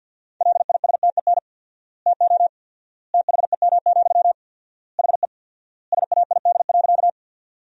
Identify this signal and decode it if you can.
Morse code, heard as a keyed tone, about 50 wpm, 700 Hz. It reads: LISTEN TO THEM, HE SAID.
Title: 4x Word-Spacing - Sentences from Top 400 Words - 50wpm